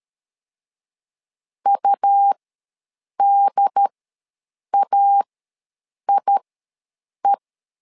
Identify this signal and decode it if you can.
Morse code, heard as a keyed tone, about 13 wpm, spaced as for 7 wpm, 800 Hz. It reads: UDAIE
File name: LEZIONI / 2 UHDN / 2 UHD ws 12wpm 1722